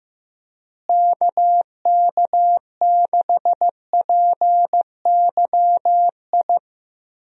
Morse code, heard as a keyed tone, about 15 wpm, 700 Hz. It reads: KK6PYI
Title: USA Call Signs - 6 Characters - Repeated 3x - 15wpm